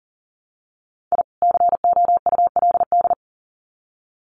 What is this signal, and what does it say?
Morse code, heard as a keyed tone, about 40 wpm, 700 Hz. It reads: I COULD
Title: Sets of 2 Words 40wpm